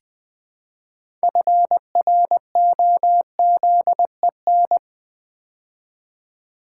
Morse code, heard as a keyed tone, about 20 wpm, 700 Hz. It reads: FROZEN
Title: Build A Word - 2 Letters At A Time - 6-Letter Words - 20wpm